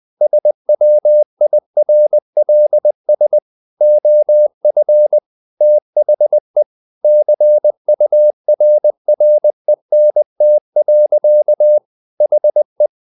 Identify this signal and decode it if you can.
Morse code, heard as a keyed tone, about 20 wpm, 600 Hz. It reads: SWIRLS OF THE CURRENT. HE
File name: the old man and the sea / the old man and the sea part 1 20wpm